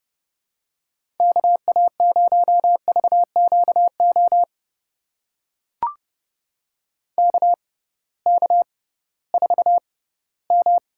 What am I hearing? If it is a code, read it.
Morse code, heard as a keyed tone, about 30 wpm, spaced as for 18 wpm, 700 Hz. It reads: KA0VQO KK4M
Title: USA Call Signs - 6x Character Spacing - 30wpm